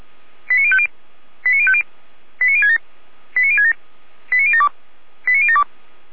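Five-tone selective call (ZVEI), heard as quick runs of five tones, tones of about 70 ms. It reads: 89050 89050 89067 89067 89072 89072
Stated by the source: ZVEI five-tone selective call received with an SDR.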